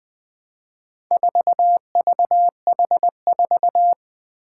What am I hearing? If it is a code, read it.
Morse code, heard as a keyed tone, about 20 wpm, 700 Hz. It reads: 4VH4
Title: Four Challenging Letter-Number Combinations 20wpm